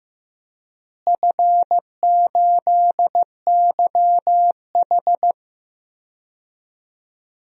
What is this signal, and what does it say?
Morse code, heard as a keyed tone, about 15 wpm, 700 Hz. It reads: F8YH
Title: DXCC-100 Call Signs - 4 Characters - Repeated 2x - 15wpm